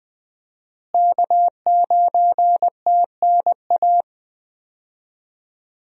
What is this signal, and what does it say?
Morse code, heard as a keyed tone, about 20 wpm, 700 Hz. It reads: K9TNA
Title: USA Call Signs - 5 Characters - Repeated 3x - 20wpm